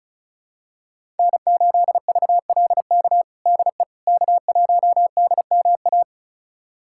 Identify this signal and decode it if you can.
Morse code, heard as a keyed tone, about 34 wpm, 700 Hz. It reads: N8VLK DE K1DMA
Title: USA Callsign DE Callsign - 5 Characters - 35wpm